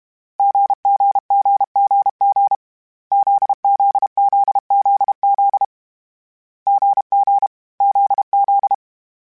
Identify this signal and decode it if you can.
Morse code, heard as a keyed tone, about 32 wpm, 800 Hz. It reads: GGGGG ZZZZZ GG ZZ
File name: CW Code Groups-mp3s / Sets mp3s / GZ GZ7 sp @30